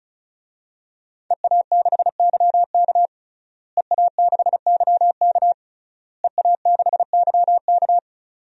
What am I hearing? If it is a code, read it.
Morse code, heard as a keyed tone, about 34 wpm, 700 Hz. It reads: EA6YK EA6YK EA6YK
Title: DXCC-100 Call Signs - 5 Characters - Repeated 3x - 35wpm